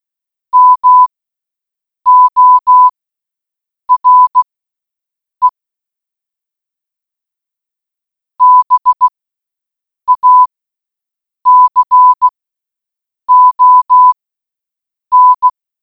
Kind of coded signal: Morse code